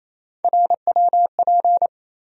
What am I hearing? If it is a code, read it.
Morse code, heard as a keyed tone, about 28 wpm, 700 Hz. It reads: RWP
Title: USA Call Sign Postfixes - 3 Letters 28wpm